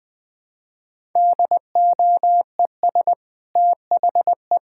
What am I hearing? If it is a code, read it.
Morse code, heard as a keyed tone, about 20 wpm, 700 Hz. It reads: DOES THE
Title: Sentences from Top 1000 Words - Review Part 5 - 20wpm